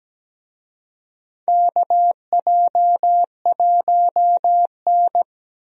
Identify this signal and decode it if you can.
Morse code, heard as a keyed tone, about 17 wpm, 700 Hz. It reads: KJ1N
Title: USA Call Signs - 4 Characters - Repeated 3x - 17wpm